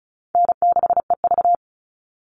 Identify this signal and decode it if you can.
Morse code, heard as a keyed tone, about 34 wpm, 700 Hz. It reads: N6EV
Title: Easy Call Signs Repeated 3x - 35wpm